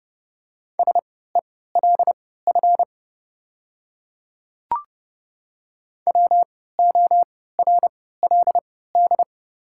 Morse code, heard as a keyed tone, about 30 wpm, spaced as for 17 wpm, 690 Hz. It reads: SELF WORLD
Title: 3x Character Spacing - Top 200 Words - 30wpm